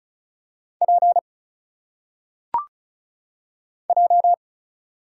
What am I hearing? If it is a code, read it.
Morse code, heard as a keyed tone, about 34 wpm, 700 Hz. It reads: P J
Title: Single Letters - Phonetic Pronunciation 35wpm